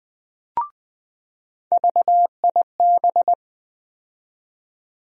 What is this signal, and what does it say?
Morse code, heard as a keyed tone, about 20 wpm, 700 Hz. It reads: VIB
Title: USA Call Sign Postfixes - 3 Letters 20wpm